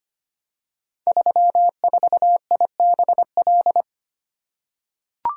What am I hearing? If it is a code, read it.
Morse code, heard as a keyed tone, about 25 wpm, 690 Hz. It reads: 34IBL
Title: Code Groups - 5 Letter:Numbers - 25wpm